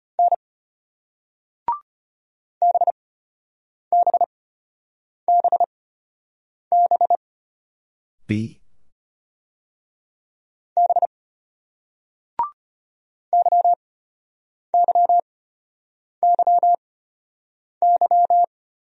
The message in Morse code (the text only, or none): N BBBB B YYYY